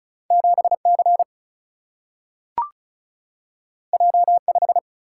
Morse code, heard as a keyed tone, about 36 wpm, 700 Hz. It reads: 7C J5
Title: Single-Letter Single-Number 35wpm